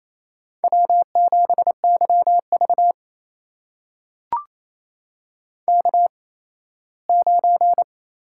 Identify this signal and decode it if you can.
Morse code, heard as a keyed tone, about 28 wpm, spaced as for 17 wpm, 700 Hz. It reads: W7YV K9
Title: USA Call Signs - 8x Character Spacing - 28wpm